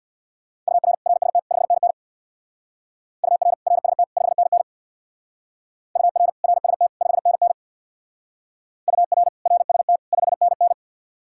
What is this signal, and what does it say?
Morse code, heard as a keyed tone, about 64 wpm, 700 Hz. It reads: UR RST 5NN UR RST 5NN UR RST 5NN UR RST 5NN